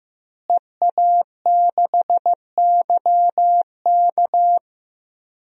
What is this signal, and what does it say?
Morse code, heard as a keyed tone, about 15 wpm, 700 Hz. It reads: EA6YK
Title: DXCC-100 Call Signs - 5 Characters - Repeated 3x - 15wpm